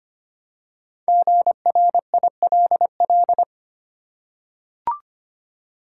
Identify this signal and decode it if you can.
Morse code, heard as a keyed tone, about 25 wpm, 700 Hz. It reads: GRILL
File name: practice / 25wpm / Easy Words with Hints - 25wpm